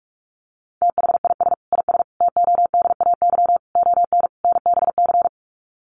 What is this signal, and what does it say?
Morse code, heard as a keyed tone, about 44 wpm, 700 Hz. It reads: THIS IS TODAY ON NBC
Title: Sets of 5 Words Encore 45wpm